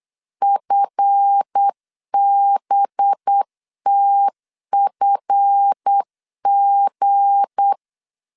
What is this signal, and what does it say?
Morse code, heard as a keyed tone, about 8 wpm, 800 Hz. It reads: FBTFG
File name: LEZIONI / 7 GF9Z / 7 GF 8wpm 3108